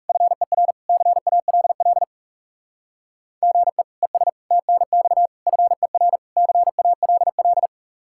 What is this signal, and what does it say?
Morse code, heard as a keyed tone, about 36 wpm, 700 Hz. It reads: FER CALL GE ES TNX FER CALL
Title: Speed-Racing Common QSO Phrases - 30wpm